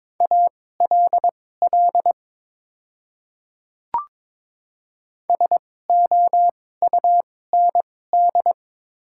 Morse code, heard as a keyed tone, about 22 wpm, spaced as for 16 wpm, 700 Hz. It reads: ALL SOUND
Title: 2x Character Spacing - Top 100 Words - 22wpm